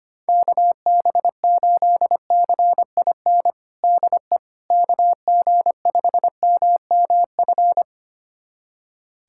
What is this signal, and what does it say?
Morse code, heard as a keyed tone, about 25 wpm, 700 Hz. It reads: KB8CIN DE KG5MMF
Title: USA Callsign DE Callsign - 6 Characters - 25wpm